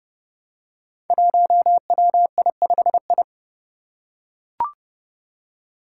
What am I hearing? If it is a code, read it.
Morse code, heard as a keyed tone, about 30 wpm, 700 Hz. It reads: 1WI5I
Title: Five Challenging Letter-Number Combinations 30wpm